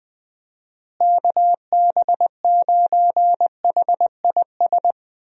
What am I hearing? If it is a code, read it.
Morse code, heard as a keyed tone, about 20 wpm, 700 Hz. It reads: KB9HIS
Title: USA Call Signs - 6 Characters - Repeated 2x - 20wpm